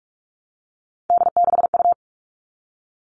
Morse code, heard as a keyed tone, about 44 wpm, 700 Hz. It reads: D6U
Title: All Cut Numbers - 3-Digit 45wpm